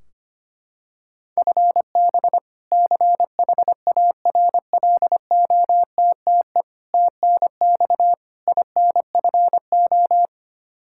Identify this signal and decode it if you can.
Morse code, heard as a keyed tone, about 25 wpm, 700 Hz. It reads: FB CHARLOTTE TNX INFO
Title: Practice QSO - Exchange 3 - More Info - 25wpm